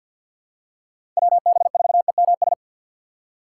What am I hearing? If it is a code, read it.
Morse code, heard as a keyed tone, about 53 wpm, 700 Hz. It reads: WB4EMS
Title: USA Call Signs - 6x Character Spacing - 50wpm